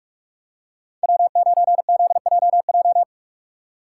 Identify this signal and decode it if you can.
Morse code, heard as a keyed tone, about 46 wpm, 700 Hz. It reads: W9ZJJ
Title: USA Call Signs - 5 Characters - 3x Character Spacing - 45wpm